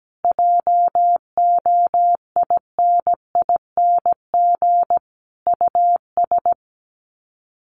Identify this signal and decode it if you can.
Morse code, heard as a keyed tone, about 17 wpm, 700 Hz. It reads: JOINING US
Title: Sets of 5 Words 17wpm